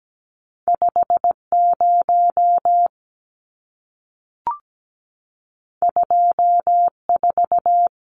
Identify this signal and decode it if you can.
Morse code, heard as a keyed tone, about 17 wpm, 700 Hz. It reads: 50 24
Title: Numbers - 2-Digit 17wpm